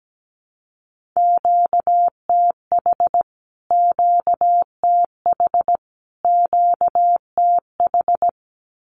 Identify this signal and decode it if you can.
Morse code, heard as a keyed tone, about 17 wpm, 700 Hz. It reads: QTH QTH QTH
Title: Q-Codes Repeated 3x 17wpm